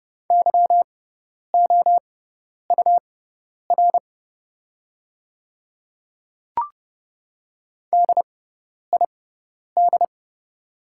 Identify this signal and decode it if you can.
Morse code, heard as a keyed tone, about 30 wpm, spaced as for 10 wpm, 700 Hz. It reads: YOUR DID
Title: 6x Character Spacing - Top 100 Words - 30wpm